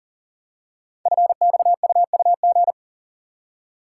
Morse code, heard as a keyed tone, about 38 wpm, 700 Hz. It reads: FXUUG
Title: Code Groups - 5 Letters - 40wpm